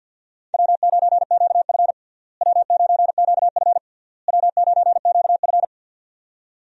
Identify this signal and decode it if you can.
Morse code, heard as a keyed tone, about 52 wpm, 700 Hz. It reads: W9QF W9QF W9QF
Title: USA Call Signs - 4 Characters - Repeated 3x - 50wpm